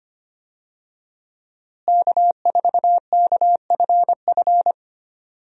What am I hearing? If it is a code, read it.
Morse code, heard as a keyed tone, about 25 wpm, 700 Hz. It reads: K4KFF
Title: USA Call Signs - 5 Characters - Repeated 3x - 25wpm